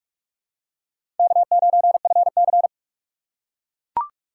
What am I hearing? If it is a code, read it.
Morse code, heard as a keyed tone, about 43 wpm, 700 Hz. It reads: K9FC
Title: USA Call Signs - 2x Character Spacing - 45wpm